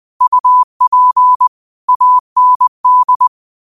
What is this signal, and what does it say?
Morse code, heard as a keyed tone, about 20 wpm, 1000 Hz. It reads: UP AND
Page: Daily quotes in Morse Code at 20 Words per Minute. Use this to improve your code copying skill.